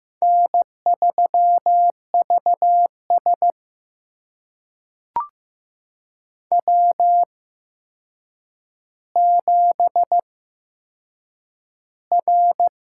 Morse code, heard as a keyed tone, about 15 wpm, spaced as for 7 wpm, 700 Hz. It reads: N3VS W7R
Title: USA Call Signs - 8x Character Spacing - 15wpm